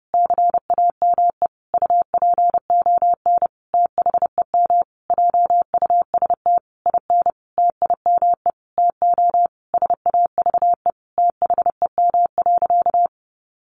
Morse code, heard as a keyed tone, about 30 wpm, 700 Hz. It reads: CAME UPON THEM JUST IN TIME TO SAVE THEM.
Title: Sentences from Top 1100 Words 30wpm